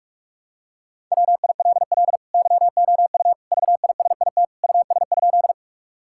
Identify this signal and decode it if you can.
Morse code, heard as a keyed tone, about 42 wpm, 700 Hz. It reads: WILL YOU VISIT US?
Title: Sentences from Top 1000 Words - Review Part 6 - 45wpm